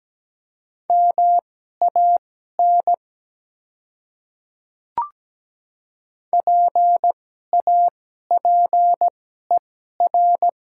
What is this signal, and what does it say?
Morse code, heard as a keyed tone, about 17 wpm, spaced as for 12 wpm, 700 Hz. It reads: MAN PAPER
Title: 2x Character Spacing - Top 300 Words - 17wpm